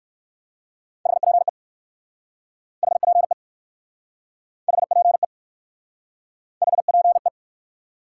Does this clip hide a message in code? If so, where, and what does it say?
Yes: Morse code from 1.1 to 7.3 s, heard as a keyed tone, about 59 wpm, 700 Hz. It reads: HPE HPE HPE HPE